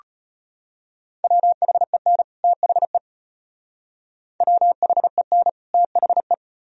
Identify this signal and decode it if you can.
Morse code, heard as a keyed tone, about 32 wpm, 700 Hz. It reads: WHEN THE WHEN THE